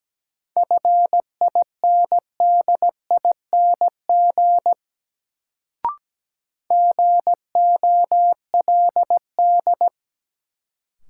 Morse code, heard as a keyed tone, about 17 wpm, 700 Hz. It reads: FINDING GOLD
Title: Sentences from Top 1100 Words - New Word Drill Alt - 17wpm